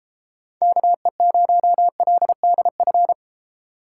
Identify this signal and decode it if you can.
Morse code, heard as a keyed tone, about 31 wpm, 700 Hz. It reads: KE0LDF